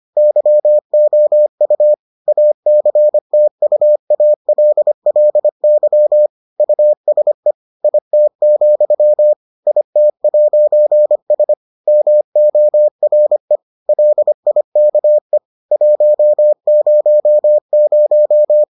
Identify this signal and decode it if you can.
Morse code, heard as a keyed tone, about 25 wpm, 600 Hz. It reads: YOU ACTUALLY USE IT, IT'S MORE LIKE 100